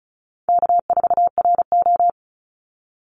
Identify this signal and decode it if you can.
Morse code, heard as a keyed tone, about 34 wpm, 700 Hz. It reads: K4RO
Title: Call Signs - 35wpm